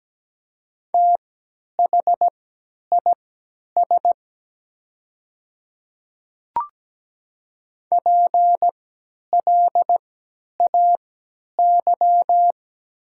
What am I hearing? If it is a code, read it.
Morse code, heard as a keyed tone, about 17 wpm, spaced as for 10 wpm, 700 Hz. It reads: THIS PLAY
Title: 3x Character Spacing - Top 200 Words - 17wpm